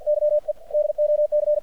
Morse code, heard as a keyed tone, about 54 wpm, 600 Hz. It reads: YE DOC
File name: archived / CRSnet